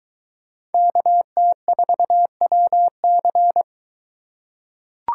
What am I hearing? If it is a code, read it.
Morse code, heard as a keyed tone, about 21 wpm, 700 Hz. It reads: KT4WC